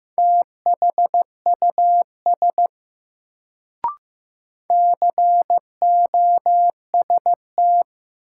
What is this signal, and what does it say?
Morse code, heard as a keyed tone, about 16 wpm, 700 Hz. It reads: THUS COST
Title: Sentences from Top 800 Words - New Word Drill - 15wpm